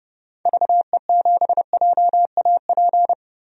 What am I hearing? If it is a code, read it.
Morse code, heard as a keyed tone, about 30 wpm, 700 Hz. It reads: VE7JAP
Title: DXCC-100 Call Signs - 6 Characters - 30wpm